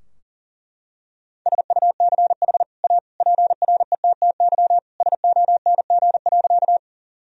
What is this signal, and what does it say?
Morse code, heard as a keyed tone, about 40 wpm, 700 Hz. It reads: SUCH A PRETTY SONG.
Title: Sentences from Top 1000 Words - Review Part 2 - 40wpm